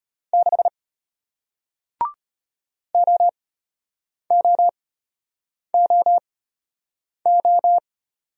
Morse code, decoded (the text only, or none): 6 OOOO